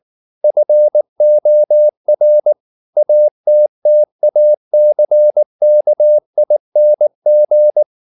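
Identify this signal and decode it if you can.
Morse code, heard as a keyed tone, about 19 wpm, 600 Hz. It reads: FOR ATTACKING
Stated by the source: Morse code at 19 wpm from CW med Gnister, reading Ken Follett's novel Never in English.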